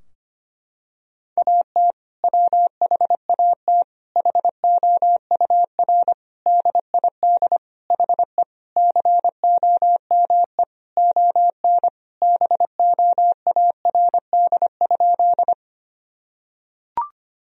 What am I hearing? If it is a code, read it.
Morse code, heard as a keyed tone, about 25 wpm, 700 Hz. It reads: AT WHAT HOUR DID HE COME ON BOARD?
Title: Sentences from Top 1000 Words - Review Part 4 - 25wpm